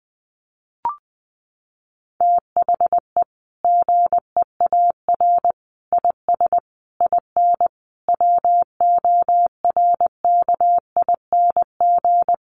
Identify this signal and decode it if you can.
Morse code, heard as a keyed tone, about 20 wpm, 700 Hz. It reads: THE GEAR IS IN WORKING
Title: Sentences from Top 1400 Words 20wpm